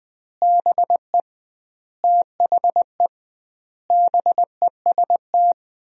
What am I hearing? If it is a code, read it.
Morse code, heard as a keyed tone, about 20 wpm, 700 Hz. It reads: BE THE BEST
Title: 2x Word-Spacing - Sentences from Top 400 Words - 20wpm